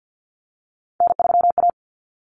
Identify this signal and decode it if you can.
Morse code, heard as a keyed tone, about 50 wpm, 700 Hz. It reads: N3A
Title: Common Cut Numbers - 3-Digit 50wpm